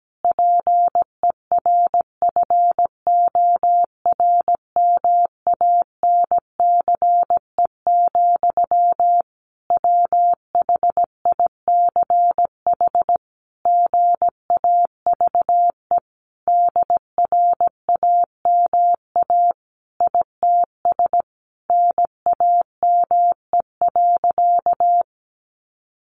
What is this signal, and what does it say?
Morse code, heard as a keyed tone, about 17 wpm, 700 Hz. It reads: PERFORMANCE, WHICH GAVE DRAMA ITS NAME.
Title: Sentences from Top 1500 Words 17wpm